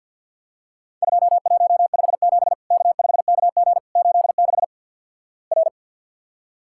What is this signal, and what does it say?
Morse code, heard as a keyed tone, about 48 wpm, 700 Hz. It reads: 2157 K5KG 86 R
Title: Worked All Europe DX Contest - QTC Exchange - 50wpm